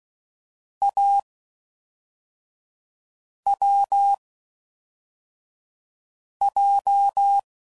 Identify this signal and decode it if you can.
Morse code, heard as a keyed tone, about 16 wpm, 800 Hz. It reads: A W J
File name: Combinations / Difficult Combos / AWJ